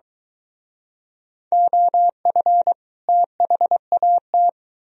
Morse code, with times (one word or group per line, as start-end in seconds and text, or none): 1.5-2.7 OF
3.1-4.5 THAT